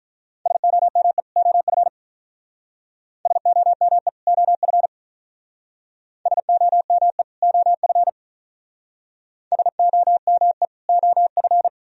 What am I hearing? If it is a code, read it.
Morse code, heard as a keyed tone, about 42 wpm, 700 Hz. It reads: SOME OF SOME OF SOME OF SOME OF
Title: Speed-Racing - Sets of 2 Words 35wpm